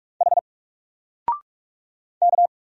Morse code, heard as a keyed tone, about 44 wpm, 700 Hz. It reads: H K
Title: Single Letter-Number 45wpm